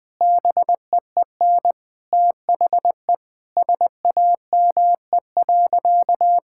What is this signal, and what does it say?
Morse code, heard as a keyed tone, about 20 wpm, 700 Hz. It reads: BEEN THE SAME.
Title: Sentences from Top 1000 Words - Review Part 6 - 20wpm